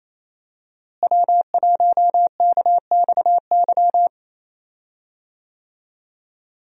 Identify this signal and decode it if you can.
Morse code, heard as a keyed tone, about 28 wpm, 700 Hz. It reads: W1KXY
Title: USA Call Signs - 5 Characters - Repeated 3x - 28wpm